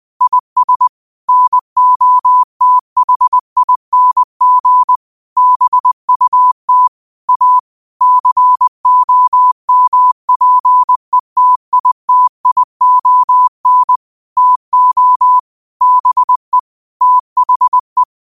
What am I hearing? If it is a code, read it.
Morse code, heard as a keyed tone, about 20 wpm, 1000 Hz. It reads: IS NOTHING BUT A COMPETITION TO BE THE
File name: qotdfiles / 20wpm